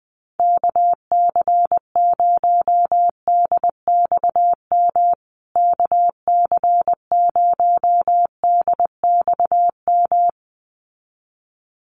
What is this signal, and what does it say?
Morse code, heard as a keyed tone, about 19 wpm, 700 Hz. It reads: KC0DXM KC0DXM
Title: USA Call Signs Repeated 2x 20wpm